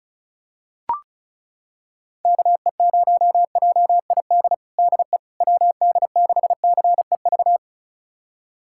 Morse code, heard as a keyed tone, about 36 wpm, 700 Hz. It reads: KE0JID DE WD6CEV
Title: USA Callsign DE Callsign - 6 Characters - 35wpm